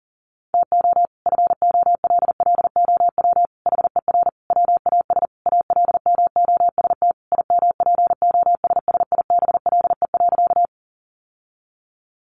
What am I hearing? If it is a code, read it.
Morse code, heard as a keyed tone, about 40 wpm, 700 Hz. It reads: TO FOLLOW HER WAS ALMOST IMPOSSIBLE.
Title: Sentences from Top 1100 Words 40wpm